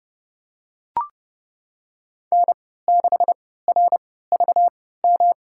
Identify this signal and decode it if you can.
Morse code, heard as a keyed tone, about 30 wpm, spaced as for 17 wpm, 700 Hz. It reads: N6RVM
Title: USA Call Signs - 5 Characters - 3x Character Spacing - 30wpm